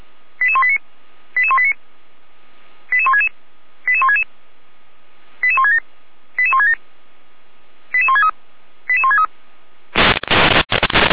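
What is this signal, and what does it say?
Five-tone selective call (ZVEI), heard as quick runs of five tones, tones of about 70 ms. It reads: 88189 88189 88166 88166 88167 88167 88163 88163
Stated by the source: ZVEI five-tone call 88191 received by SDR.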